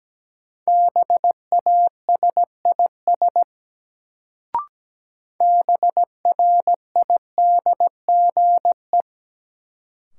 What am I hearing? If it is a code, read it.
Morse code, heard as a keyed tone, about 17 wpm, 700 Hz. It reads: BASIS BRIDGE
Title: Sentences from Top 1200 Words - New Word Drill Alt - 17wpm